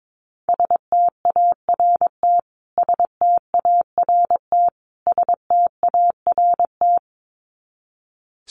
Morse code, heard as a keyed tone, about 22 wpm, 700 Hz. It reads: START START START
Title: Top 500 Words Repeated 22wpm